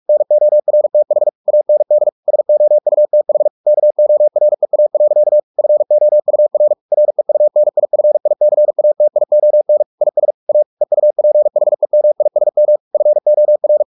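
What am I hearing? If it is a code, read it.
Morse code, heard as a keyed tone, about 44 wpm, 600 Hz. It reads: NORTH AND SOUTH KOREA. FOUR REUNIFICATION IS A EUPHEMISM FOR
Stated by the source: Morse practice at 45 wpm, the novel Never by Ken Follett in English, from CW med Gnister.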